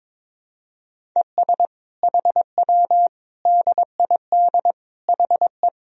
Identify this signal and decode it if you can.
Morse code, heard as a keyed tone, about 21 wpm, 700 Hz. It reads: ES HW DID HE
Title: Sentences from Top 400 Words and Abbreviations 22wpm